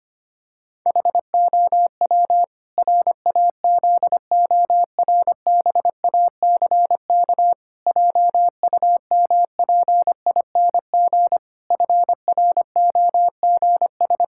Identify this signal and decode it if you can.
Morse code, heard as a keyed tone, about 25 wpm, 700 Hz. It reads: HOW RAZORBACK JUMPING FROGS
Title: Pangram Challenge v2 - 25wpm